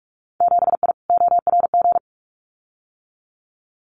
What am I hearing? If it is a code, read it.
Morse code, heard as a keyed tone, about 46 wpm, 700 Hz. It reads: 7I ORG
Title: Winter Field Day Exchanges Encore 45wpm